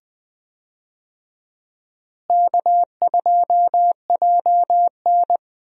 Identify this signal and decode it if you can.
Morse code, heard as a keyed tone, about 20 wpm, 700 Hz. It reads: K2JN
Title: USA Call Signs - 4 Characters - Repeated 2x - 20wpm